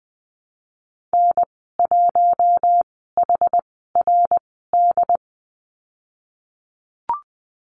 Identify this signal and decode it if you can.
Morse code, heard as a keyed tone, about 20 wpm, spaced as for 14 wpm, 700 Hz. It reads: N1HRD
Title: 069 Lesson 12g - Letter d - Call Signs 20wpm